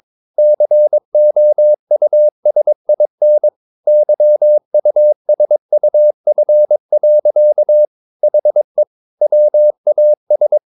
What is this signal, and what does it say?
Morse code, heard as a keyed tone, about 22 wpm, 600 Hz. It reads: COUSIN YUSUF. HE WAS
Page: Romanen Never af Ken Follet på engelsk.